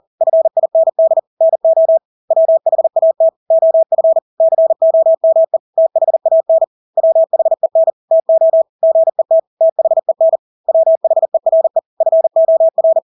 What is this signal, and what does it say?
Morse code, heard as a keyed tone, about 40 wpm, 650 Hz. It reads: FIND NO WHAT OF COME THAN WHEN TO GET THEN WHERE FOR
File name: Engelske ord 40-40 wpm